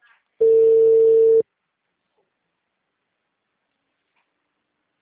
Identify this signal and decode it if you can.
Ringback tone (China, 450 Hz).
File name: out- -2012- - - .39